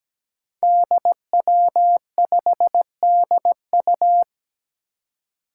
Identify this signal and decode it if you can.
Morse code, heard as a keyed tone, about 17 wpm, 700 Hz. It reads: DW5DU
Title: Five Challenging Letter-Number Combinations 17wpm